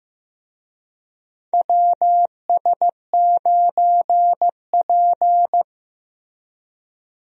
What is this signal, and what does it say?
Morse code, heard as a keyed tone, about 15 wpm, 700 Hz. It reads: WS9P
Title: USA Call Signs - 4 Characters - Repeated 2x - 15wpm